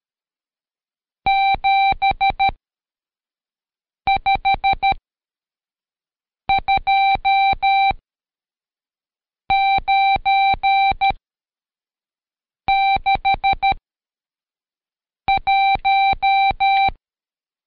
Morse code, heard as a keyed tone, about 13 wpm, 780 Hz. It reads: 7 5 2 9 6 1
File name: Code Quick / AudioDownld2 / Seg99-Day11-CheckTest 11